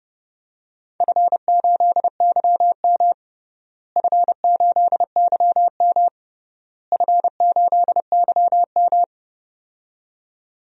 Morse code, heard as a keyed tone, about 29 wpm, 700 Hz. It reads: F8YM F8YM F8YM
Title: DXCC-100 Call Signs - 4 Characters - Repeated 3x - 30wpm